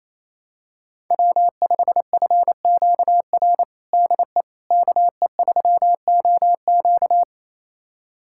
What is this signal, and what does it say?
Morse code, heard as a keyed tone, about 28 wpm, 700 Hz. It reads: W5FQR DE KE3OQ
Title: USA Callsign DE Callsign - 5 Characters - 28wpm